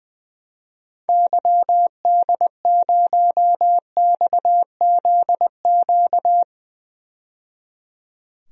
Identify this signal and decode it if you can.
Morse code, heard as a keyed tone, about 20 wpm, 700 Hz. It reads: YD0XZQ
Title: DXCC-100 Call Signs - 6 Characters - 20wpm